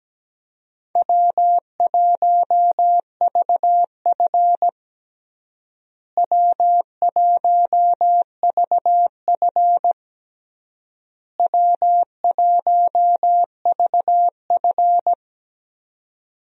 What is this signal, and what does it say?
Morse code, heard as a keyed tone, about 17 wpm, 700 Hz. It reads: W1VF W1VF W1VF
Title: USA Call Signs - 4 Characters - Repeated 3x - 17wpm